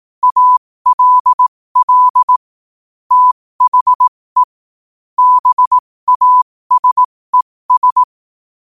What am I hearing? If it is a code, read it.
Morse code, heard as a keyed tone, about 18 wpm, spaced as for 15 wpm, 1000 Hz. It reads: ALL THE BASES
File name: qotdfiles / 15wpm